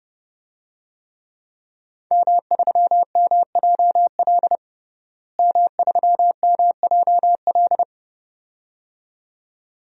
Morse code, heard as a keyed tone, about 30 wpm, 700 Hz. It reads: M3MJL M3MJL
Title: DXCC-100 Call Signs - 5 Characters - Repeated 2x - 30wpm